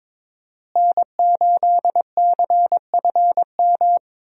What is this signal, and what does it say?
Morse code, heard as a keyed tone, about 22 wpm, 700 Hz. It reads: N8CFM
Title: USA Call Signs - 5 Characters - Repeated 3x - 22wpm